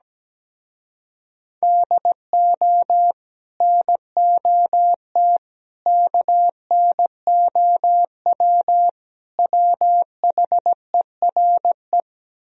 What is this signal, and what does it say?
Morse code, heard as a keyed tone, about 17 wpm, 700 Hz. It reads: DO NOT KNOW WHERE
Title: Speed-Racing - Sets of 4 Words 17wpm